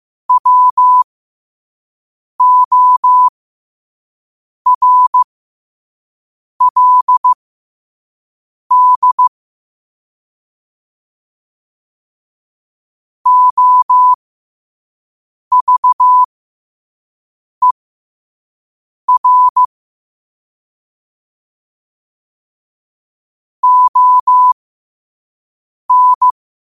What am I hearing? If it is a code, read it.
Morse code, heard as a keyed tone, about 15 wpm, spaced as for 5 wpm, 1000 Hz. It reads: WORLD OVER ON
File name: qotdfiles / 5wpm